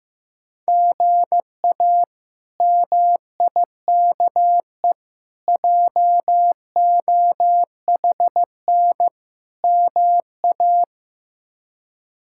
Morse code, heard as a keyed tone, about 15 wpm, 700 Hz. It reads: GA MIKE JOHN MA
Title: K1USN SST QSO Practice - 15wpm